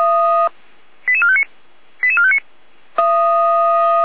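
Five-tone selective call (ZVEI), heard as quick runs of five tones, tones of about 70 ms. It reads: 88459 88459
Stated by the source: ZVEI five-tone call 88352 received by SDR on channel 455.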